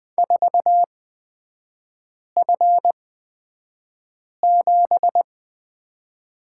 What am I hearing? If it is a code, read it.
Morse code, heard as a keyed tone, about 20 wpm, spaced as for 5 wpm, 700 Hz. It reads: 4F7
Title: 224 Lesson 36e - BT - Three Characters 20wpm